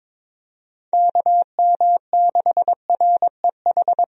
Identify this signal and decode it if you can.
Morse code, heard as a keyed tone, about 22 wpm, 700 Hz. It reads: KM6REH
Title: USA Call Signs - 6 Characters - Repeated 2x - 22wpm